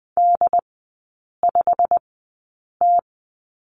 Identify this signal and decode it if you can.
Morse code, heard as a keyed tone, about 20 wpm, spaced as for 8 wpm, 700 Hz. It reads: D5T
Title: 138 Lesson 23e - Letter f - Three Characters 20wpm